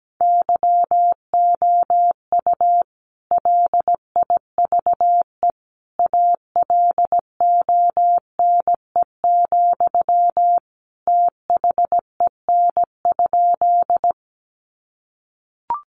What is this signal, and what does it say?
Morse code, heard as a keyed tone, about 17 wpm, 700 Hz. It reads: YOU LIVE ALONE, THEN?
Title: Sentences from Top 1000 Words 17wpm